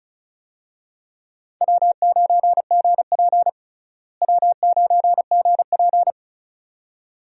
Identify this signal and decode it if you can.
Morse code, heard as a keyed tone, about 36 wpm, 700 Hz. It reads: W9GP W9GP
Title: USA Call Signs - 4 Characters - Repeated 2x - 35wpm